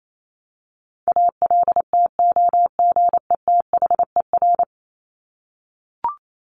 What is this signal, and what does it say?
Morse code, heard as a keyed tone, about 28 wpm, 700 Hz. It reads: ALTOGETHER
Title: Top 1501-1750 Words 28wpm